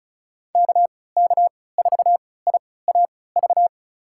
Morse code, heard as a keyed tone, about 36 wpm, spaced as for 20 wpm, 700 Hz. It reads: KK4IAV
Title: USA Call Signs - 3x Character Spacing - 35wpm